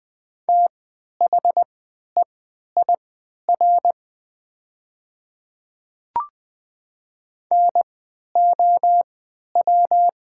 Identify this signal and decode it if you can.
Morse code, heard as a keyed tone, about 20 wpm, spaced as for 11 wpm, 700 Hz. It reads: THEIR NOW
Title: 3x Character Spacing - Top 100 Words - 20wpm